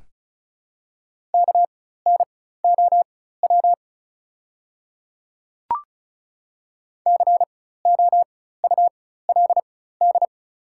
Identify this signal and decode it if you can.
Morse code, heard as a keyed tone, about 36 wpm, spaced as for 16 wpm, 700 Hz. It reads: KNOW COULD
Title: 4x Character Spacing - Top 100 Words - 35wpm